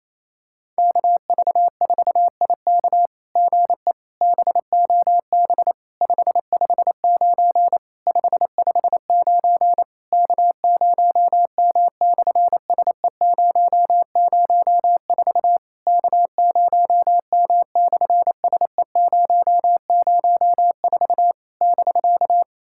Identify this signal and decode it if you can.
Morse code, heard as a keyed tone, about 28 wpm, 700 Hz. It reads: KV4IK GE BOB 559 559 K0M/SE004 K0M/SE004 <BK>
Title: SOTA QSO Practice 28wpm